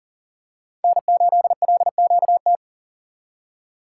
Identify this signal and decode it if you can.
Morse code, heard as a keyed tone, about 40 wpm, 700 Hz. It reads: N8LQT
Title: USA Call Signs - 5 Characters - 6x Character Spacing - 40wpm